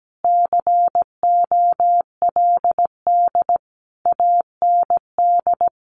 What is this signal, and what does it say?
Morse code, heard as a keyed tone, about 17 wpm, 700 Hz. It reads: COLD AND
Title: Sentences from Top 400 Words 17wpm